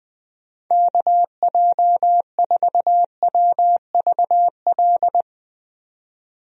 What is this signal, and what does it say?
Morse code, heard as a keyed tone, about 20 wpm, 700 Hz. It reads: KJ4WVL
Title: USA Call Signs - 6 Characters - 20wpm